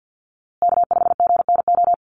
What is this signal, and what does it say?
Morse code, heard as a keyed tone, about 50 wpm, 700 Hz. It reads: K5GNO
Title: USA Call Signs Repeated 2x 50wpm